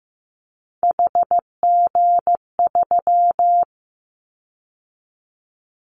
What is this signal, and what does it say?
Morse code, heard as a keyed tone, about 15 wpm, 700 Hz. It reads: HG3
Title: Callsign Prefixes for Top 50 DXCC - 15wpm